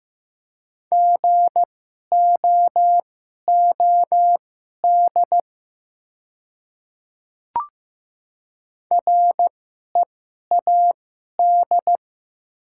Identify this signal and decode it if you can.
Morse code, heard as a keyed tone, about 15 wpm, spaced as for 11 wpm, 700 Hz. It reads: GOOD READ
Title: 2x Character Spacing - Top 200 Words - 15wpm